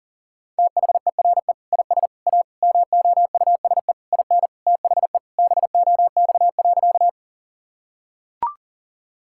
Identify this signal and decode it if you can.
Morse code, heard as a keyed tone, about 40 wpm, 700 Hz. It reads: THERE IS A MOUSE IN THE BOX.
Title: Sentences from Top 1000 Words - Review Part 5 - 40wpm